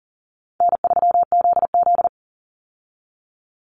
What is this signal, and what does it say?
Morse code, heard as a keyed tone, about 40 wpm, 700 Hz. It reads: N3ZZ
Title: Call Signs Repeated 2x - 40wpm